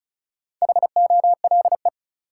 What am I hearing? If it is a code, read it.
Morse code, heard as a keyed tone, about 36 wpm, 700 Hz. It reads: HOLE
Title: Sentences from Top 1600 Words - New Word Drill - 35wpm